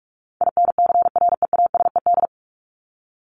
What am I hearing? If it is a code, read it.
Morse code, heard as a keyed tone, about 44 wpm, 700 Hz. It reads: INCREASED
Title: ED Suffix - Top 100 Words with Base Words 45wpm